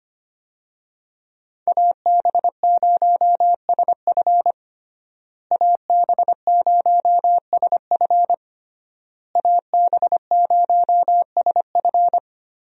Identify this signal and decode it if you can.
Morse code, heard as a keyed tone, about 25 wpm, 700 Hz. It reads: AB0SF AB0SF AB0SF
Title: USA Call Signs - 5 Characters - Repeated 3x - 25wpm